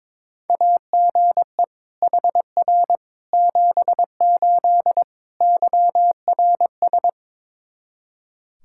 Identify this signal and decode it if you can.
Morse code, heard as a keyed tone, about 22 wpm, 700 Hz. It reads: AGE HR 78 YRS
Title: Practice QSO - Exchange 3 - More Info - 22wpm